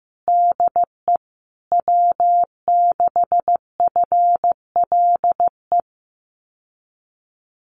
Morse code, heard as a keyed tone, about 15 wpm, 700 Hz. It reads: DE W6FLE
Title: Common QSO Phrases Encore 15wpm